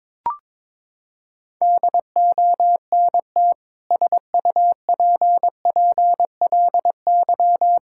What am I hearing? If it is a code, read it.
Morse code, heard as a keyed tone, about 22 wpm, 700 Hz. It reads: DONT SUPPLY
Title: Sentences from Top 1000 Words - Review Part 5 - 22wpm